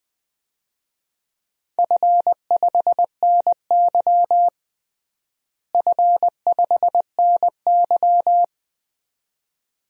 Morse code, heard as a keyed tone, about 20 wpm, 700 Hz. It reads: F5NY F5NY
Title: DXCC-100 Call Signs - 4 Characters - Repeated 2x - 20wpm